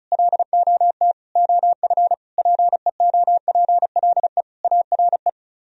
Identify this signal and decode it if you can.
Morse code, heard as a keyed tone, about 44 wpm, 700 Hz. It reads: LOT OF PEOPLE ARE